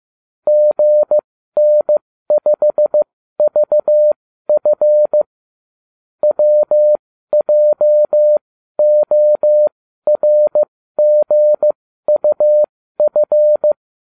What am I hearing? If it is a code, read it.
Morse code, heard as a keyed tone, about 15 wpm, spaced as for 12 wpm, 600 Hz. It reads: GN5VF WJORGUF